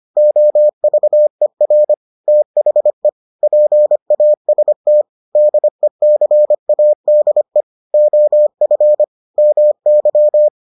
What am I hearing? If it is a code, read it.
Morse code, heard as a keyed tone, about 25 wpm, 600 Hz. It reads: OVER THE PAST DECADE OF MY